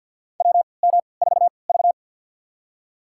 Morse code, heard as a keyed tone, about 51 wpm, spaced as for 28 wpm, 700 Hz. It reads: WM4V
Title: USA Call Signs - 4 Characters - 3x Character Spacing - 50wpm